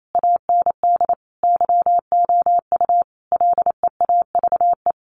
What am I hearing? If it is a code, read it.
Morse code, heard as a keyed tone, about 28 wpm, 700 Hz. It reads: AND YOU LEAVE
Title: A Sound Of Thunder 28wpm